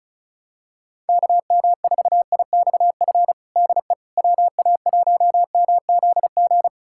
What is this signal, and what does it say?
Morse code, heard as a keyed tone, about 36 wpm, 700 Hz. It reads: KM4IXF DE WA1MZG
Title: USA Callsign DE Callsign 35wpm